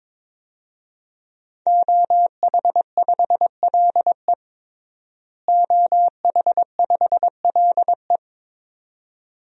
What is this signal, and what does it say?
Morse code, heard as a keyed tone, about 22 wpm, 700 Hz. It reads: OH5LE OH5LE
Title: DXCC-100 Call Signs - 5 Characters - Repeated 2x - 22wpm